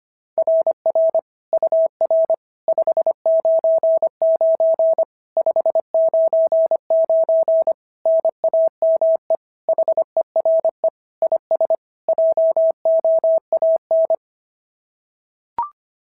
Morse code, heard as a keyed tone, about 25 wpm, 690 Hz. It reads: RR UR 599 599 NAME HERE IS JOAN
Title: SOTA QSO Practice 25wpm